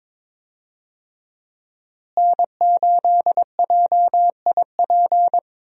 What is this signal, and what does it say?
Morse code, heard as a keyed tone, about 22 wpm, 700 Hz. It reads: N8JIP
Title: USA Call Signs - 5 Characters - Repeated 2x - 22wpm